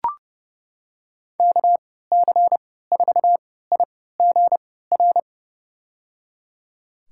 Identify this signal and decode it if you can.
Morse code, heard as a keyed tone, about 30 wpm, spaced as for 17 wpm, 700 Hz. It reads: KC4IGR
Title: USA Call Signs - 6 Characters - 3x Character Spacing - 30wpm